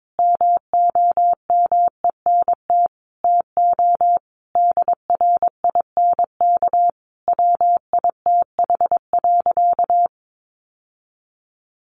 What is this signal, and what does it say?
Morse code, heard as a keyed tone, about 22 wpm, 700 Hz. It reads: MOMENT TO DRINK WITH.
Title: Sentences from Top 1500 Words 22wpm